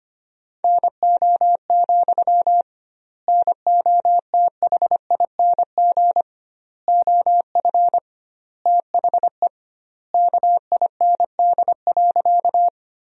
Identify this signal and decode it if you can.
Morse code, heard as a keyed tone, about 25 wpm, 700 Hz. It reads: NO, NOTHING OF THE KIND.
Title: 2x Word-Spacing - Sentences from Top 500 Words - 25wpm